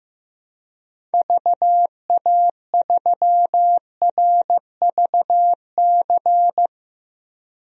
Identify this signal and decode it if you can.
Morse code, heard as a keyed tone, about 15 wpm, 700 Hz. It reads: VA3RVC
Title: DXCC-100 Call Signs - 6 Characters - 15wpm